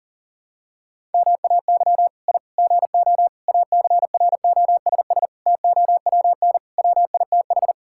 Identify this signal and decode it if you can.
Morse code, heard as a keyed tone, about 40 wpm, 700 Hz. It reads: MAY I GO ACROSS TOWN WITH
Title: Sentences from Top 1000 Words - Review Part 5 - 40wpm